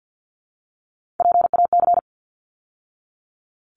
Morse code, heard as a keyed tone, about 48 wpm, 700 Hz. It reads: PAC
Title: ARRL Sections 50wpm